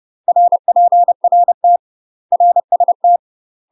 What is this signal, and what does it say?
Morse code, heard as a keyed tone, about 30 wpm, 700 Hz. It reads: RPRT RST
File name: UK MASTER QSOs TRAINING / 30WPM / UK MASTER QSOs 0019